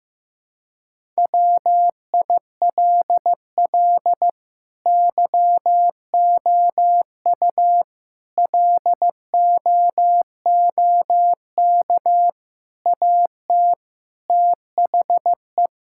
Morse code, heard as a keyed tone, about 15 wpm, 700 Hz. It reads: WILL YOU LOOK AT THE
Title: Sentences from Top 1000 Words - Review Part 5 - 15wpm